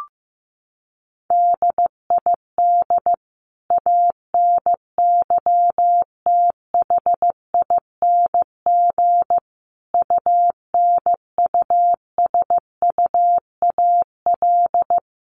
Morse code, heard as a keyed tone, about 15 wpm, 700 Hz. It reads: DID ANYTHING UNUSUAL
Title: Sentences from Top 1400 Words 15wpm